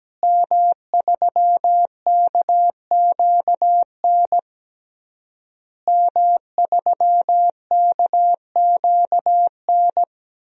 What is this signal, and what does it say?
Morse code, heard as a keyed tone, about 17 wpm, 700 Hz. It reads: M3KQN M3KQN
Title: DXCC-100 Call Signs - 5 Characters - Repeated 3x - 17wpm